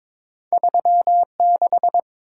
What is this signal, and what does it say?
Morse code, heard as a keyed tone, about 22 wpm, 700 Hz. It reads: 36
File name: practice / 22wpm / Two Letter-Number Combinations - 22wpm